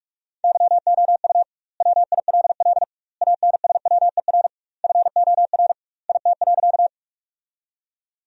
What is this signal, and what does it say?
Morse code, heard as a keyed tone, about 43 wpm, 700 Hz. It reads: YOU WILL ANSWER FOR IT.
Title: 2x Word-Spacing - Sentences from Top 300 Words - 45wpm